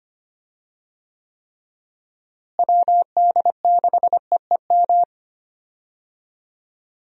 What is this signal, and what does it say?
Morse code, heard as a keyed tone, about 25 wpm, 700 Hz. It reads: WD6EEM
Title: USA Call Signs - 6 Characters - Repeated 2x - 25wpm